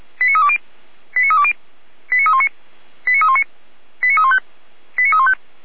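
Five-tone selective call (ZVEI), heard as quick runs of five tones, tones of about 70 ms. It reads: 89320 89320 89319 89319 89315 89315
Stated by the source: Received on an SDR, ZVEI five-tone selective call.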